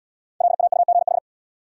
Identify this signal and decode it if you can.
Morse code, heard as a keyed tone, about 77 wpm, 700 Hz. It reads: LAUGH